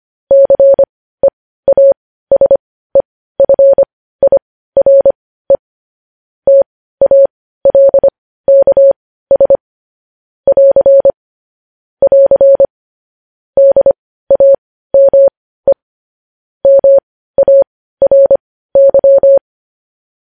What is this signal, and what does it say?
Morse code, heard as a keyed tone, about 25 wpm, spaced as for 15 wpm, 550 Hz. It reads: CEASEFIRE TALKS <AR> <AR> DAME MARY
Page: news-headlines-in-morse-code-at-15-wpm